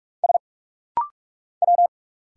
Morse code, heard as a keyed tone, about 46 wpm, 700 Hz. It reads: S W
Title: Single Letter-Number Alt - Rapid-Fire 45wpm